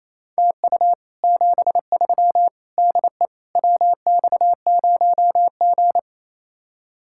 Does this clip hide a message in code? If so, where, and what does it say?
Yes: Morse code from 0.4 to 6.0 s, heard as a keyed tone, about 36 wpm, 700 Hz. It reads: TU 73 DE WX0G